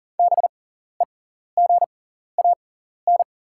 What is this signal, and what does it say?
Morse code, heard as a keyed tone, about 38 wpm, spaced as for 14 wpm, 700 Hz. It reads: BEGAN
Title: 6x Character Spacing - Top 300 Words - 40wpm